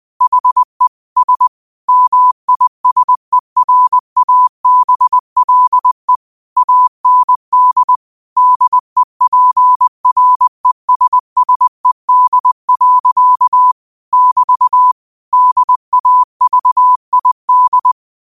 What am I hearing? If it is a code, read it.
Morse code, heard as a keyed tone, about 20 wpm, 1000 Hz. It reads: HE S MISERABLE AND DEPRESSED. <BT> DAVID
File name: qotdfiles / 20wpm